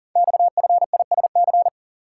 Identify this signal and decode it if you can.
Morse code, heard as a keyed tone, about 40 wpm, 700 Hz. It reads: XFISC
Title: Code Groups - 5 Letters - 40wpm